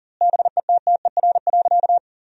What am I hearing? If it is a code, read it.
Morse code, heard as a keyed tone, about 40 wpm, 700 Hz. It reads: BETTER.
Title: 2x Word-Spacing - Sentences from Top 500 Words - 40wpm